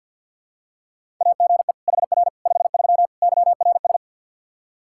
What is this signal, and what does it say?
Morse code, heard as a keyed tone, about 51 wpm, 700 Hz. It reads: AGE HR 53 YRS
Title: Practice QSO - Exchange 3 - More Info - 50wpm